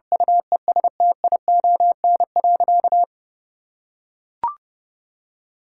Morse code, heard as a keyed tone, about 29 wpm, 700 Hz. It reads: UESTION.
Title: Sentences from Top 1000 Words - Review Part 6 - 30wpm